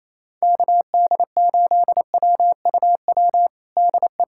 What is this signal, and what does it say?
Morse code, heard as a keyed tone, about 28 wpm, 700 Hz. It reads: KD8WUW DE
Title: USA Callsign DE Callsign - 6 Characters - 28wpm